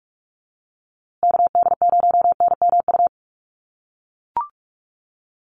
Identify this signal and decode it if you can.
Morse code, heard as a keyed tone, about 44 wpm, 700 Hz. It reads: KD0NMU
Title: USA Call Signs Repeated 2x 45wpm